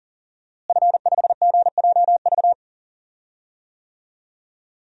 Morse code, heard as a keyed tone, about 40 wpm, 700 Hz. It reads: F5GJV
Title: DXCC-100 Call Signs - 5 Characters - Repeated 3x - 40wpm